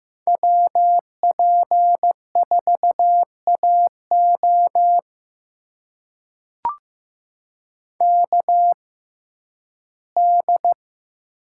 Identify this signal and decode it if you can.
Morse code, heard as a keyed tone, about 15 wpm, spaced as for 11 wpm, 700 Hz. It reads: WP4AO KD
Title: USA Call Signs - 5 Characters - 6x Character Spacing - 15wpm